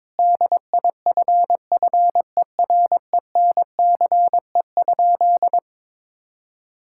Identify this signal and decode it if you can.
Morse code, heard as a keyed tone, about 22 wpm, 700 Hz. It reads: DIFFERENCE?
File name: practice / 22wpm / Sentences with 2-Word Phrases - 22wpm